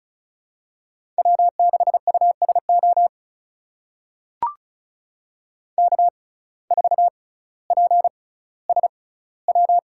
Morse code, heard as a keyed tone, about 36 wpm, spaced as for 18 wpm, 700 Hz. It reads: W6USO K4PSW
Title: USA Call Signs - 5 Characters - 6x Character Spacing - 35wpm